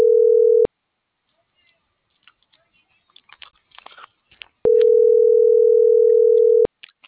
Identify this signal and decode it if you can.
Ringback tone (North America), 440 + 480 Hz.
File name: CENTRAL TERRESTRE / out cris.f. - .47163